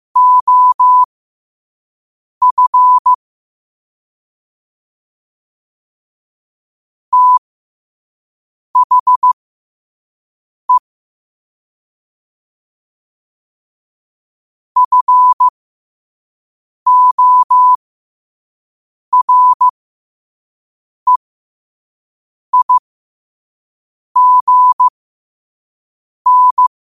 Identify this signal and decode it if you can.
Morse code, heard as a keyed tone, about 15 wpm, spaced as for 5 wpm, 1000 Hz. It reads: OF THE FOREIGN
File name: qotdfiles / 5wpm